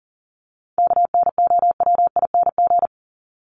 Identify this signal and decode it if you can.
Morse code, heard as a keyed tone, about 40 wpm, 700 Hz. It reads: KNOWING
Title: ING Suffix - Top 100 Words with Base Words 40wpm